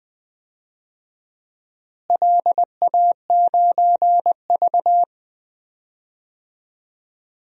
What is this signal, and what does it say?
Morse code, heard as a keyed tone, about 20 wpm, 700 Hz. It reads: LA9V
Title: DXCC-100 Call Signs - 4 Characters - Repeated 3x - 20wpm